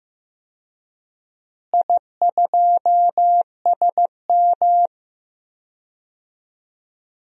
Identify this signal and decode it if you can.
Morse code, heard as a keyed tone, about 15 wpm, 700 Hz. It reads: I2SM
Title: DXCC-100 Call Signs - 4 Characters - Repeated 3x - 15wpm